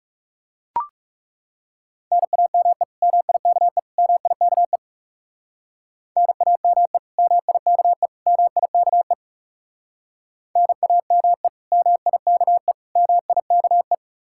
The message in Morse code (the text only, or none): NAME MIKE MIKE NAME MIKE MIKE NAME MIKE MIKE